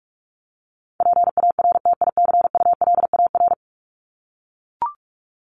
Morse code, heard as a keyed tone, about 44 wpm, 700 Hz. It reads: PARTICULAR
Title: Top 501-750 Words 45wpm